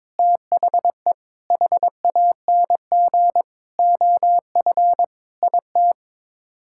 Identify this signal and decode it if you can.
Morse code, heard as a keyed tone, about 22 wpm, 700 Hz. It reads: THE HANG OF IT
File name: practice / 22wpm / Sets of 7 Words - 22wpm